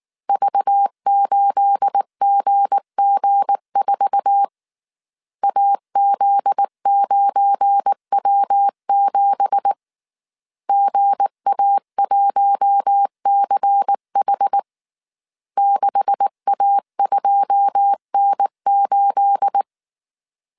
Morse code, heard as a keyed tone, about 19 wpm, 800 Hz. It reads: V8GG4 AZ9W7 GA1CH 6A2N8
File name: CW / DET / 1 9 1500C S A 18wpm 2459